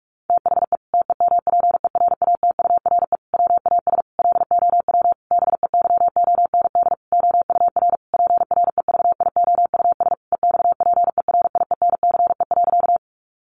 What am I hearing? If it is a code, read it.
Morse code, heard as a keyed tone, about 44 wpm, 700 Hz. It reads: THE TEMPERATURE WAS LOW BEYOND OUR PREVIOUS EXPERIENCE.
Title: Sentences from Top 1500 Words 45wpm